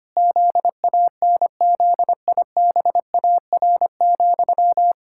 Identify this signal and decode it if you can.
Morse code, heard as a keyed tone, about 25 wpm, 700 Hz. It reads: ZANZIBAR,
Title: Pangram Challenge v2 - 25wpm